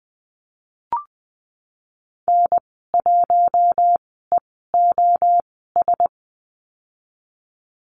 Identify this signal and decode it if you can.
Morse code, heard as a keyed tone, about 20 wpm, spaced as for 15 wpm, 700 Hz. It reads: N1EOS
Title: 088 Lesson 15g - Number 5 - Call Signs 20wpm